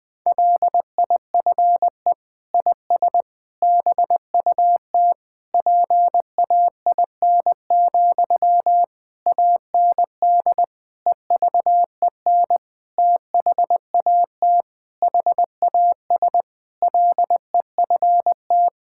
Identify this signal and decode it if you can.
Morse code, heard as a keyed tone, about 20 wpm, 700 Hz. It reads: LIFE IS BUT PAIN, AND EVEN THAT HAS LEFT
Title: Sentences from Top 1600 Words 20wpm